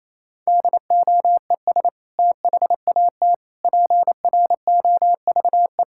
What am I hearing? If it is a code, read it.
Morse code, heard as a keyed tone, about 28 wpm, 700 Hz. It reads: DOES THAT PROVE
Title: Sentences from Top 1600 Words 28wpm